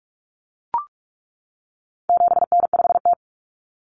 Morse code, heard as a keyed tone, about 44 wpm, 700 Hz. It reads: 7N5T
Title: Common Cut Numbers - 4-Digit 45wpm